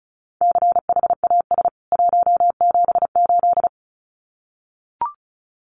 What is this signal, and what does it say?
Morse code, heard as a keyed tone, about 34 wpm, 700 Hz. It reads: CHAS 178
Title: CWT Exchange 35wpm